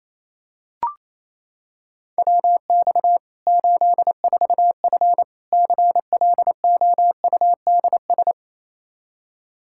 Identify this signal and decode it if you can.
Morse code, heard as a keyed tone, about 28 wpm, 700 Hz. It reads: WX 84F CLOUDS
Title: Practice QSO - Exchange 2 - Info - 28wpm